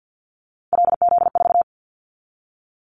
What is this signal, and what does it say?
Morse code, heard as a keyed tone, about 50 wpm, 700 Hz. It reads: LZ4
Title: Callsign Prefixes for Top 50 DXCC - 50wpm